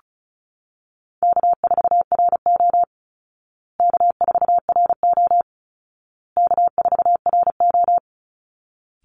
Morse code, heard as a keyed tone, about 34 wpm, 700 Hz. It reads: K4RO K4RO K4RO
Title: Easy Call Signs Repeated 3x - 35wpm